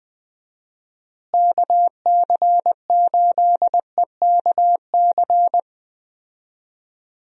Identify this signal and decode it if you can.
Morse code, heard as a keyed tone, about 20 wpm, 700 Hz. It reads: KC8EKC
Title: USA Call Signs - 6 Characters - Repeated 2x - 20wpm